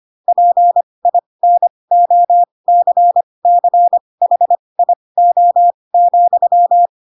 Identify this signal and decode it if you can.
Morse code, heard as a keyed tone, about 25 wpm, spaced as for 20 wpm, 700 Hz. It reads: PINOCCHIO,
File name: Pinocchio / Pinocchio ch4 15 25 ik1hge